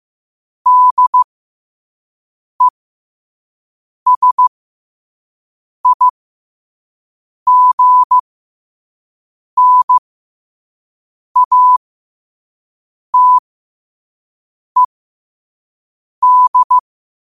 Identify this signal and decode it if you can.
Morse code, heard as a keyed tone, about 15 wpm, spaced as for 5 wpm, 1000 Hz. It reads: DESIGNATED
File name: qotdfiles / 5wpm